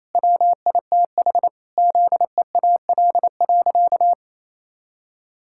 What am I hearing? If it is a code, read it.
Morse code, heard as a keyed tone, about 28 wpm, 700 Hz. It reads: WITH ZEAL.
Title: Pangram Challenge v2 - 28wpm